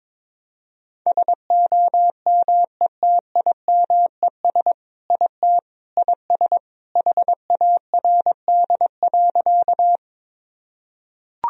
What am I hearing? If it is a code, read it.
Morse code, heard as a keyed tone, about 22 wpm, 700 Hz. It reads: SOMETIMES IT IS HARD.
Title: Sentences from Top 1000 Words - Review Part 3 - 22wpm